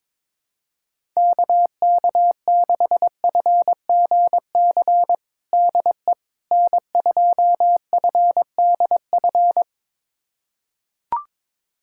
Morse code, heard as a keyed tone, about 22 wpm, 700 Hz. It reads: KK6FGC DE N2FDF
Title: USA Callsign DE Callsign 22wpm